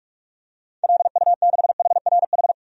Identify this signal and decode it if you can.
Morse code, heard as a keyed tone, about 43 wpm, 700 Hz. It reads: LU6HRH
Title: DXCC-100 Call Signs - 6 Characters - 45wpm